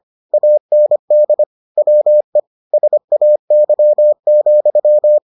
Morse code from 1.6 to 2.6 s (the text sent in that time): WE